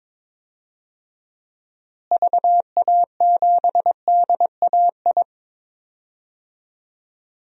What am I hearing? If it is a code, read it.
Morse code, heard as a keyed tone, about 22 wpm, 700 Hz. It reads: VA7DAI
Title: DXCC-100 Call Signs - 6 Characters - Repeated 3x - 22wpm